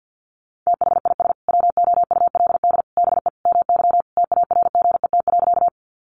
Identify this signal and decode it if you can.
Morse code, heard as a keyed tone, about 49 wpm, 700 Hz. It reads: THIS WOULD BE MY TARGET.
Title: Sentences from Top 1200 Words 50wpm